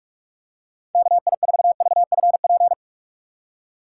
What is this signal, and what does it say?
Morse code, heard as a keyed tone, about 45 wpm, 700 Hz. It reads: KI4VFP
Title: USA Call Signs - 6x Character Spacing - 45wpm